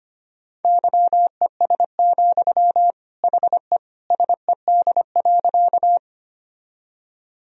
Morse code, heard as a keyed tone, about 25 wpm, 700 Hz. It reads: YES, HE SED.
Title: Sentences from Top 500 Words and Abbreviations 25wpm